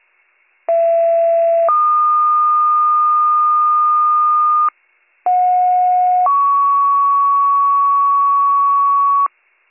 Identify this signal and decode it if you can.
Two-tone page, heard as a one-second tone, then a three-second tone, about 670 then 1220 Hz, about 730 then 1120 Hz. It is heard from 0.7 to 9.3 s.